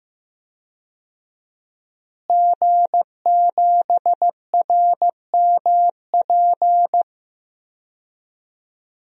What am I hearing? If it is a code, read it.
Morse code, heard as a keyed tone, about 15 wpm, 700 Hz. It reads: G7RMP
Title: DXCC-100 Call Signs - 5 Characters - Repeated 3x - 15wpm